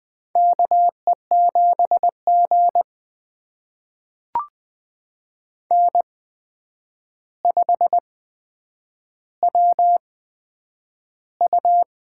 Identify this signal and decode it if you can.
Morse code, heard as a keyed tone, about 20 wpm, spaced as for 9 wpm, 700 Hz. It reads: KE7G N5WU
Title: USA Call Signs - 8x Character Spacing - 20wpm